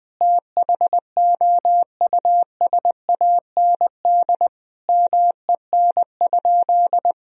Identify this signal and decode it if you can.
Morse code, heard as a keyed tone, about 20 wpm, 700 Hz. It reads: THOUSAND MEN?
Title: Sentences from Top 1000 Words - Review Part 3 - 20wpm